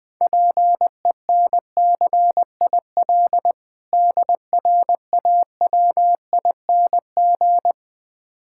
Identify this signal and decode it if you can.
Morse code, heard as a keyed tone, about 20 wpm, 700 Hz. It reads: PENCIL DRAWING
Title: Difficult Words - Speed-Racing with Hints - 20wpm